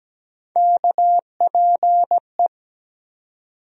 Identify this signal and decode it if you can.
Morse code, heard as a keyed tone, about 17 wpm, 700 Hz. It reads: KPE
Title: USA Call Sign Postfixes - 3 Letters 17wpm